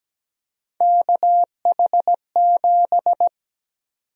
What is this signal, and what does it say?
Morse code, heard as a keyed tone, about 17 wpm, 700 Hz. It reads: KH7
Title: USA Call Sign Prefixes - 17wpm